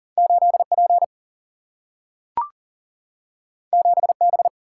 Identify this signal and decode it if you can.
Morse code, heard as a keyed tone, about 39 wpm, 700 Hz. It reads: 8 P 7 B
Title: Single-Letter Single-Number 40wpm